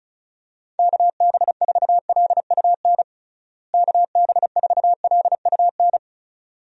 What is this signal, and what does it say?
Morse code, heard as a keyed tone, about 36 wpm, 700 Hz. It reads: KB4LUN KB4LUN
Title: USA Call Signs - 6 Characters - Repeated 2x - 35wpm